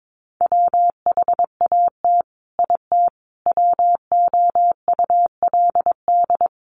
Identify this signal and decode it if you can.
Morse code, heard as a keyed tone, about 22 wpm, 700 Hz. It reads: WHAT IT WOULD
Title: Sets of 5 Words Encore 22wpm